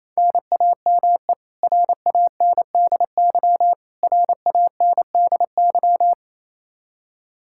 Morse code, heard as a keyed tone, about 28 wpm, 700 Hz. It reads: NAME RANDY RANDY
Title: Practice QSO - Exchange 1 - Beginning - 28wpm